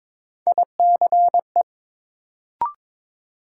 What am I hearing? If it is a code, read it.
Morse code, heard as a keyed tone, about 22 wpm, 700 Hz. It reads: ICE
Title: Sentences from Top 1000 Words - New Word Drill - 22wpm